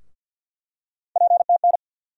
Morse code, heard as a keyed tone, about 49 wpm, 700 Hz. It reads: PTN
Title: USA Call Sign Postfixes - 3 Letters 50wpm